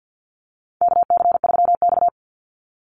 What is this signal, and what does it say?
Morse code, heard as a keyed tone, about 50 wpm, 700 Hz. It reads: KC3X
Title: Call Signs Repeated 3x - 50wpm